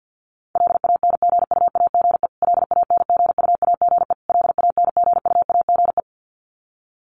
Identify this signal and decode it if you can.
Morse code, heard as a keyed tone, about 48 wpm, 700 Hz. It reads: LANGUAGE LANGUAGE LANGUAGE
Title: Top 500 Words Repeated 50wpm